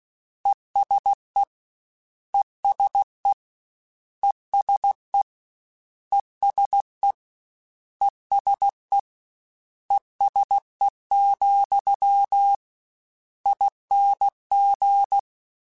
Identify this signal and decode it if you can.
Morse code, heard as a keyed tone, about 16 wpm, 800 Hz. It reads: ESE ESE ESE ESE ESE ESE, ING